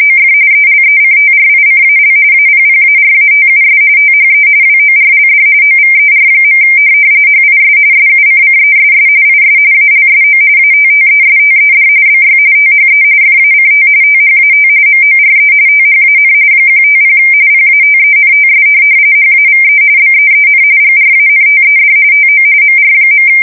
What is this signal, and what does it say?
RTTY, about 45.45 baud. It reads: OF ELECTRICAL APPARATUS THE WORLD OVER. HIS SUCCESS IS ALSO ATTESTED BY THE FACT THAT MORE THAN 150 PATENTS HAVE BEEN GRANTED TO MR.